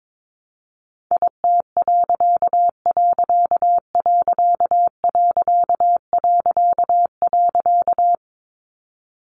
Morse code, heard as a keyed tone, about 22 wpm, 700 Hz. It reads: IT......
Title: A Sound Of Thunder 22wpm